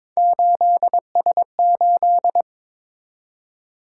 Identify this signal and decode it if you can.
Morse code, heard as a keyed tone, about 26 wpm, 700 Hz. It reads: 8S8
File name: Speed-Racing - Three Letter-Number Combinations - 22wpm